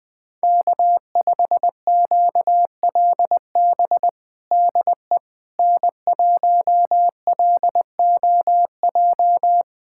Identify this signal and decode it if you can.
Morse code, heard as a keyed tone, about 20 wpm, 700 Hz. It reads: K5QLB DE N1LOJ
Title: USA Callsign DE Callsign - 5 Characters - 20wpm